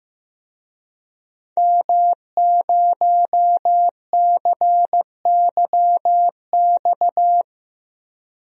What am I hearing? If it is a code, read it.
Morse code, heard as a keyed tone, about 15 wpm, 700 Hz. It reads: M0CYX
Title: DXCC-100 Call Signs - 5 Characters - Repeated 3x - 15wpm